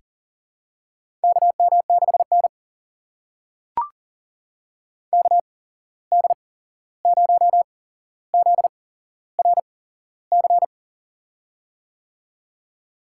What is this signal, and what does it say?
Morse code, heard as a keyed tone, about 40 wpm, spaced as for 15 wpm, 700 Hz. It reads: KM6N KD0ZRC
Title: USA Call Signs - 8x Character Spacing - 40wpm